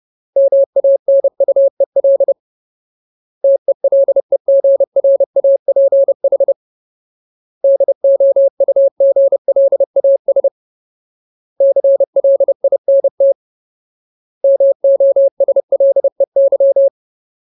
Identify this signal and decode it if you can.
Morse code, heard as a keyed tone, about 30 wpm, 550 Hz. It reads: MANUEL TELEGRAPH DOUGLAS CLINT MOSLEY
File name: Ultimate Word List / 30WPM Ultimate